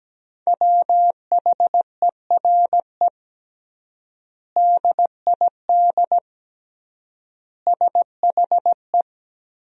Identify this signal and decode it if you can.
Morse code, heard as a keyed tone, about 17 wpm, 700 Hz. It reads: WHERE DID SHE
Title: 3x Word-Spacing - Sentences from Top 200 Words - 17wpm